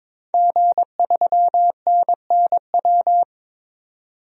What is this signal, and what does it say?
Morse code, heard as a keyed tone, about 22 wpm, 700 Hz. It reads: G3NNW
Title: DXCC-100 Call Signs - 5 Characters - 22wpm